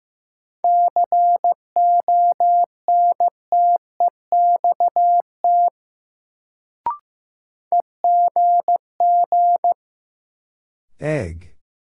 Morse code, heard as a keyed tone, about 15 wpm, 700 Hz. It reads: CONTEXT EGG
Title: Sentences from Top 1000 Words - Review Part 7 - New Word Drill - 15wpm